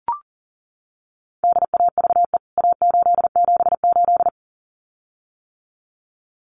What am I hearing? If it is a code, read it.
Morse code, heard as a keyed tone, about 40 wpm, 700 Hz. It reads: DAVE A878
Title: CWT Exchange 40wpm